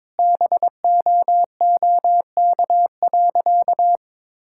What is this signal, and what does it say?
Morse code, heard as a keyed tone, about 22 wpm, 700 Hz. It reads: BOOK.
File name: Sentences with 7-Word Phrases - 22wpm